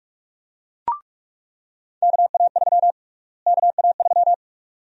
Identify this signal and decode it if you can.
Morse code, heard as a keyed tone, about 46 wpm, 700 Hz. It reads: KA3 KA3
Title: USA Call Sign Prefixes - Repeated 2x - 45wpm